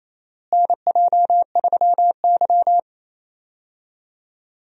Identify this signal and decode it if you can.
Morse code, heard as a keyed tone, about 28 wpm, 700 Hz. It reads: NJ3Y
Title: USA Call Signs - 4 Characters - Repeated 3x - 28wpm